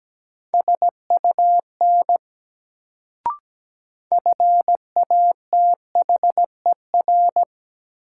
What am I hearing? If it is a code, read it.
Morse code, heard as a keyed tone, about 17 wpm, 700 Hz. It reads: SUN FATHER
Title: Sentences from Top 300 Words - New Word Drill - 17wpm